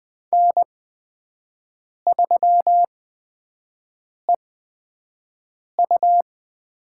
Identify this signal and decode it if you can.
Morse code, heard as a keyed tone, about 20 wpm, spaced as for 5 wpm, 700 Hz. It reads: N3EU
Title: USA Call Signs - 5 Characters - 8x Character Spacing - 20wpm